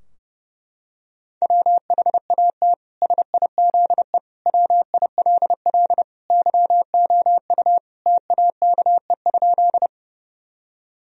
Morse code, heard as a keyed tone, about 32 wpm, 700 Hz. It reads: WHAT SIZE WILL YOU TAKE?
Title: Sentences from Top 1000 Words - Review Part 2 - 30wpm